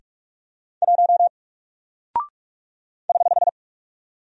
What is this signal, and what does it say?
Morse code, heard as a keyed tone, about 45 wpm, 700 Hz. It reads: 1 <HH>
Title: Single Character (Letter:Number:Punctuation:Prosign) Alt 45wpm